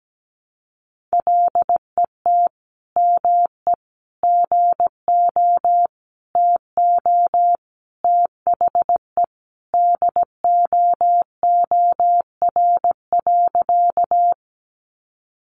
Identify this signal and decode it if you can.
Morse code, heard as a keyed tone, about 17 wpm, 700 Hz. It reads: LET ME GO TO THE DOOR.
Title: Sentences from Top 300 Words 17wpm